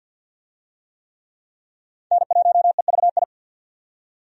Morse code, heard as a keyed tone, about 53 wpm, 700 Hz. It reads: N1EVI
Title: USA Call Signs - 5 Characters - 50wpm